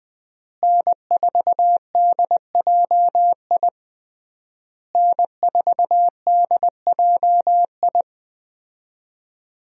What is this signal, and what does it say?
Morse code, heard as a keyed tone, about 20 wpm, 700 Hz. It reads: N4DJI N4DJI
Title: USA Call Signs - 5 Characters - Repeated 3x - 20wpm